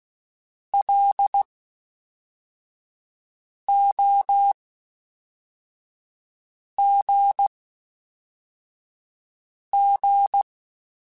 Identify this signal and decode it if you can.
Morse code, heard as a keyed tone, about 16 wpm, 800 Hz. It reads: L O G G
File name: Special INGs / Special INGs-mp3s / Fast ing @10